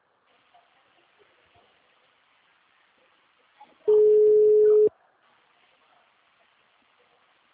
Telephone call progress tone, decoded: ringback tone

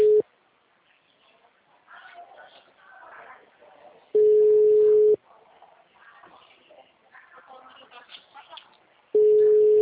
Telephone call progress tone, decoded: ringback tone

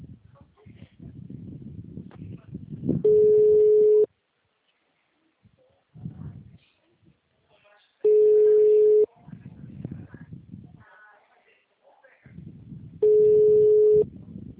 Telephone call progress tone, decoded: ringback tone

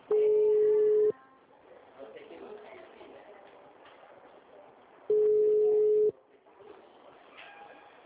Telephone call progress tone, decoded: ringback tone

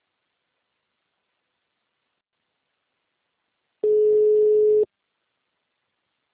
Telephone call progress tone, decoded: ringback tone

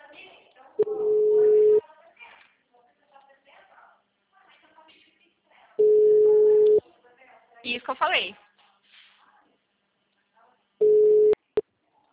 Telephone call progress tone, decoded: ringback tone

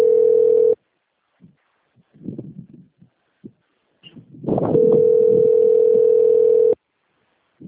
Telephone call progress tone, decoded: ringback tone

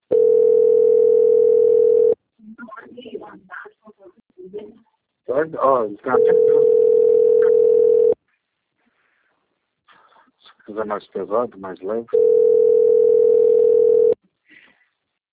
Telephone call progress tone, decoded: ringback tone

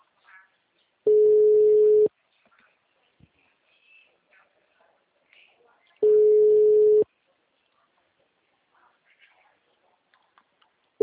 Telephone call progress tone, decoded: ringback tone

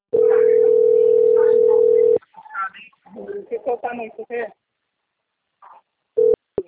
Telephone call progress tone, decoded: ringback tone

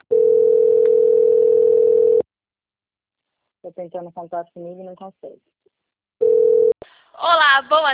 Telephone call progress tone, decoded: ringback tone

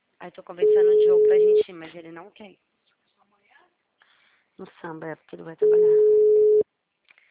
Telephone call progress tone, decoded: ringback tone

